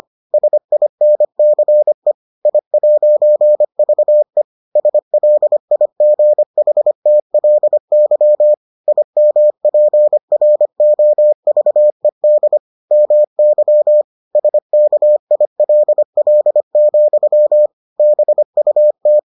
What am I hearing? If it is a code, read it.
Morse code, heard as a keyed tone, about 25 wpm, 600 Hz. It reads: SINCE I'VE SLIGHTLY IMPROVED MY SKILL, BUT